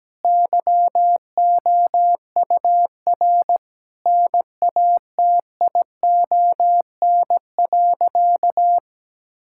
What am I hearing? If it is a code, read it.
Morse code, heard as a keyed tone, about 17 wpm, 700 Hz. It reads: YOUR NATION.
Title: Sentences from Top 1000 Words - Review Part 6 - 17wpm